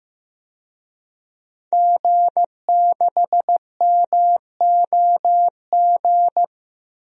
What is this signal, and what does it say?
Morse code, heard as a keyed tone, about 15 wpm, 700 Hz. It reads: G6MOG
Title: DXCC-100 Call Signs - 5 Characters - Repeated 3x - 15wpm